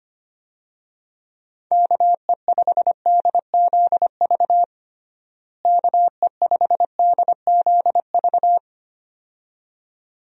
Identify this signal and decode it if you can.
Morse code, heard as a keyed tone, about 25 wpm, 700 Hz. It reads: KE5DZV KE5DZV
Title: USA Call Signs - 6 Characters - Repeated 2x - 25wpm